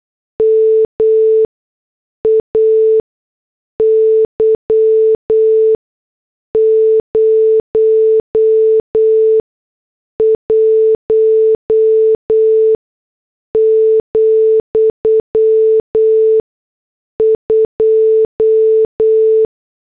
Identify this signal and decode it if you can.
Morse code, heard as a keyed tone, about 8 wpm, spaced as for 6 wpm, 440 Hz. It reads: MAY 01, 2